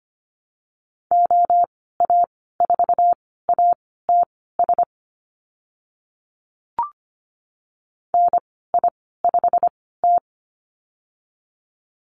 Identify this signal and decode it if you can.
Morse code, heard as a keyed tone, about 25 wpm, spaced as for 16 wpm, 700 Hz. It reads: OA4ATS NI5T
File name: Lesson 15g - Number 5 - Call Signs 25wpm